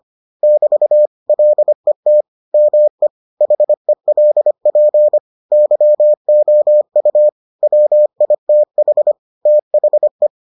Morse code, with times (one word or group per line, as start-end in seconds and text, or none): 0.4-1.1 <BT>
1.3-2.2 LET
2.5-3.1 ME
3.4-5.2 HELP
5.5-7.3 YOU
7.6-9.1 WITH
9.5-10.3 THE